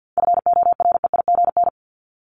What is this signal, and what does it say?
Morse code, heard as a keyed tone, about 50 wpm, 700 Hz. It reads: FOREIGN
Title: Top 1251-1500 Words 50wpm